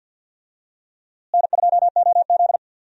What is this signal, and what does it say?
Morse code, heard as a keyed tone, about 52 wpm, 700 Hz. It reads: N2OZ
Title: USA Call Signs - 4 Characters - Repeated 2x - 50wpm